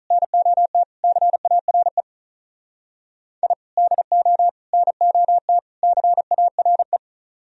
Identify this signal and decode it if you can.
Morse code, heard as a keyed tone, about 44 wpm, 700 Hz. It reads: NOT CARE I DO NOT CARE